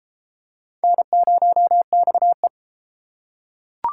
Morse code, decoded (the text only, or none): N0XE